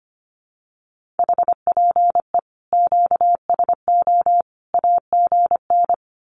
Morse code, heard as a keyed tone, about 25 wpm, 700 Hz. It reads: HPE QSO AGN
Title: Common QSO Phrases 25wpm